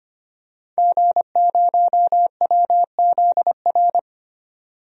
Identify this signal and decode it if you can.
Morse code, heard as a keyed tone, about 25 wpm, 700 Hz. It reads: G0WZR
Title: DXCC-100 Call Signs - 5 Characters - 25wpm